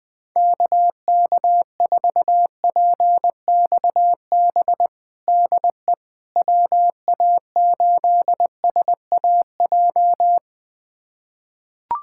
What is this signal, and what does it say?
Morse code, heard as a keyed tone, about 20 wpm, 700 Hz. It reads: KK4PXB DE WA8SAJ
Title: USA Callsign DE Callsign - 6 Characters - 20wpm